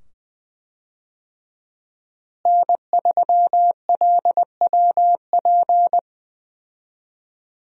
Morse code, heard as a keyed tone, about 20 wpm, 700 Hz. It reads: N3LWP
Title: USA Call Signs - 5 Characters - Repeated 2x - 20wpm